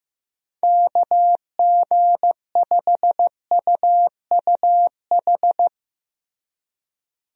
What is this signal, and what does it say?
Morse code, heard as a keyed tone, about 15 wpm, 700 Hz. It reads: KG5UUH
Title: USA Call Signs - 6 Characters - Repeated 3x - 15wpm